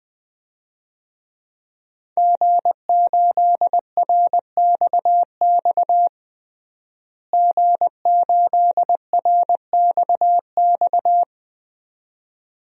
Morse code, heard as a keyed tone, about 20 wpm, 700 Hz. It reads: G8RXX G8RXX
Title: DXCC-100 Call Signs - 5 Characters - Repeated 2x - 20wpm